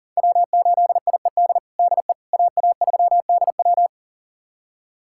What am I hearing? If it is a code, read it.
Morse code, heard as a keyed tone, about 37 wpm, 700 Hz. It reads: W8IED DE AA3DW
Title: USA Callsign DE Callsign - 5 Characters - 40wpm